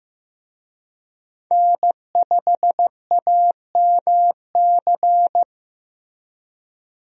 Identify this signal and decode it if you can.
Morse code, heard as a keyed tone, about 15 wpm, 700 Hz. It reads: N5AMC
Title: USA Call Signs - 5 Characters - Repeated 2x - 15wpm